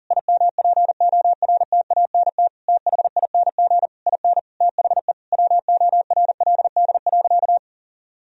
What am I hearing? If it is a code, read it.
Morse code, heard as a keyed tone, about 40 wpm, 700 Hz. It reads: IMPORTANT THING IN THE WORLD.
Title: Sentences from Top 1000 Words - Review Part 3 - 40wpm